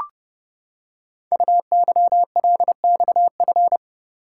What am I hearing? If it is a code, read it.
Morse code, heard as a keyed tone, about 30 wpm, 700 Hz. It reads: UYLXF
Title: Code Groups - 5 Letters - 30wpm